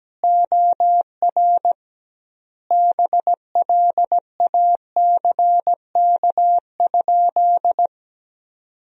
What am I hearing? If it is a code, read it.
Morse code, heard as a keyed tone, about 17 wpm, 700 Hz. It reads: OR BLACK?
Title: 2x Word-Spacing - Sentences from Top 400 Words - 17wpm